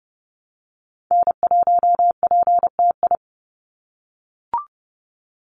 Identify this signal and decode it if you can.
Morse code, heard as a keyed tone, about 30 wpm, 700 Hz. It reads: N1PTI
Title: USA Call Signs 30wpm